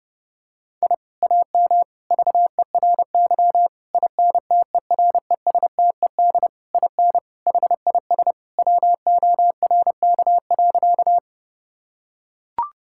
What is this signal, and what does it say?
Morse code, heard as a keyed tone, about 30 wpm, 700 Hz. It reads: I AM VERY INTERESTED IN HIS WORK.
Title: Sentences from Top 1000 Words - Review Part 6 - 30wpm